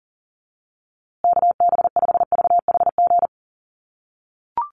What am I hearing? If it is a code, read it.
Morse code, heard as a keyed tone, about 40 wpm, 700 Hz. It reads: KB5VHG
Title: USA Call Signs Repeated 3x 40wpm